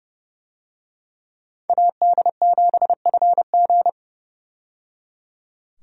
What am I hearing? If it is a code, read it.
Morse code, heard as a keyed tone, about 30 wpm, 700 Hz. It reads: AD7FG
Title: USA Call Signs - 5 Characters - 30wpm